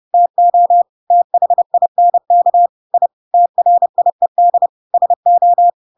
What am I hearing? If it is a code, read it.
Morse code, heard as a keyed tone, about 30 wpm, 700 Hz. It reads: TO THINK I TRIED SO
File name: Pinocchio / Pinocchio ch3 30 30 ik1hge